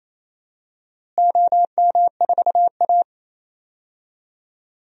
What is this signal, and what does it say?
Morse code, heard as a keyed tone, about 28 wpm, 700 Hz. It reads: OM4A
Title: DXCC-100 Call Signs - 4 Characters - Repeated 2x - 28wpm